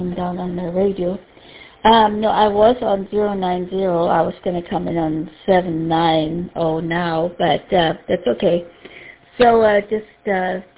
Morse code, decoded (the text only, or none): none